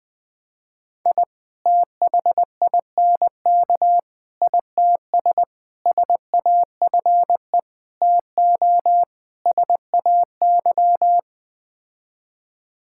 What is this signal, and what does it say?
Morse code, heard as a keyed tone, about 20 wpm, 700 Hz. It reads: I THINK ITS SAFE TO SAY
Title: Sets of 6 Words - 20wpm